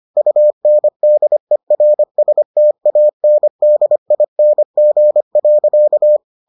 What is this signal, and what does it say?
Morse code, heard as a keyed tone, about 25 wpm, 600 Hz. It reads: UNDERSTANDING.